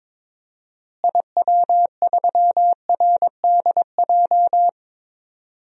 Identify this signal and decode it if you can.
Morse code, heard as a keyed tone, about 22 wpm, 700 Hz. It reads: IW3RDJ
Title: DXCC-100 Call Signs - 6 Characters - Repeated 2x - 22wpm